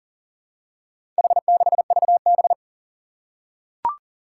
Morse code, heard as a keyed tone, about 40 wpm, 700 Hz. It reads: H6VB
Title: Four Challenging Letter-Number Combinations 40wpm